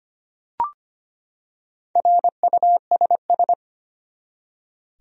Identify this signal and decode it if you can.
Morse code, heard as a keyed tone, about 25 wpm, 700 Hz. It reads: RUSS
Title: USA Common Names - Max 4 Letters - 25wpm